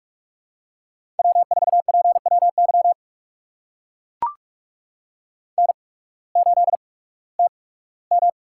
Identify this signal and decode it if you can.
Morse code, heard as a keyed tone, about 44 wpm, spaced as for 21 wpm, 700 Hz. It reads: W4PWY N8TM
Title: USA Call Signs - 8x Character Spacing - 45wpm